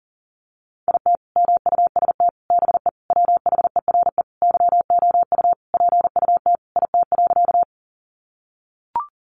Morse code, heard as a keyed tone, about 37 wpm, 700 Hz. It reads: IT MUST BE WHERE YOU PUT IT.
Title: Sentences from Top 200 Words 40wpm